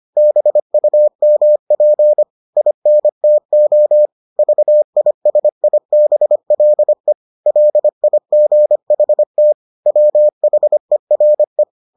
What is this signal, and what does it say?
Morse code, heard as a keyed tone, about 25 wpm, 600 Hz. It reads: BUMP INTO VISIBLE LIGHT WHERE